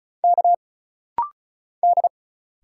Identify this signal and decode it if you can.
Morse code, heard as a keyed tone, about 35 wpm, 700 Hz. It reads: K D
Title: Single Letter-Number - Rapid-Fire 35wpm